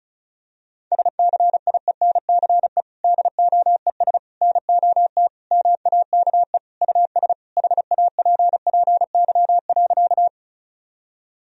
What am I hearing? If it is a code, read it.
Morse code, heard as a keyed tone, about 34 wpm, 700 Hz. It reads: SCIENCE DOES NOT MAKE US HAPPY.
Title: Sentences from Top 1000 Words - Review Part 2 - 35wpm